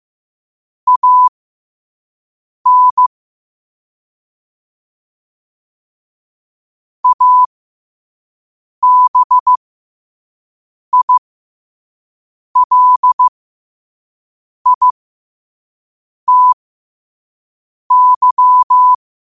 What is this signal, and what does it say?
Morse code, heard as a keyed tone, about 15 wpm, spaced as for 5 wpm, 1000 Hz. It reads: AN ABILITY
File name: qotdfiles / 5wpm